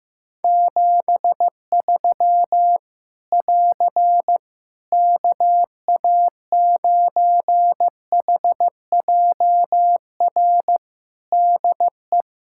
Morse code, heard as a keyed tone, about 15 wpm, 700 Hz. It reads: 73 <AR> KA9HJR DE
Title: Practice QSO - Exchange 4 - Ending - 15wpm